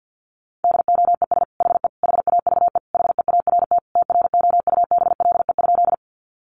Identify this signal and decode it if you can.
Morse code, heard as a keyed tone, about 50 wpm, 700 Hz. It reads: DOES HE HAVE HEART TROUBLE?
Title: Sentences from Top 1200 Words 50wpm